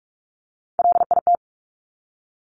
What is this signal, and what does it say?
Morse code, heard as a keyed tone, about 46 wpm, 690 Hz. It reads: LIT
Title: Top Three Letter Combinations in Words 45wpm